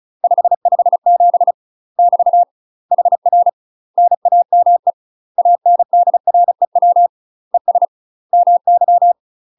Morse code, heard as a keyed tone, about 34 wpm, 700 Hz. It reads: 557 <BT> HR NAME ANDREW ES MY
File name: UK MASTER QSOs TRAINING / 35WPM / UK MASTER QSOs 0011